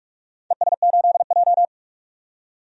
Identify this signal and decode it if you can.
Morse code, heard as a keyed tone, about 44 wpm, 700 Hz. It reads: ES8J
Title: DXCC-100 Call Signs - 4 Characters - 45wpm